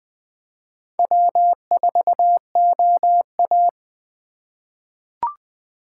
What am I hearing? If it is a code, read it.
Morse code, heard as a keyed tone, about 20 wpm, 700 Hz. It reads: W4OA
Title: USA Call Signs - 4x Character Spacing - 20wpm